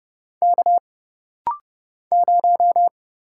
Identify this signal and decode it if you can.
Morse code, heard as a keyed tone, about 30 wpm, 700 Hz. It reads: K 0
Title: Single Letter-Number - Rapid-Fire 30wpm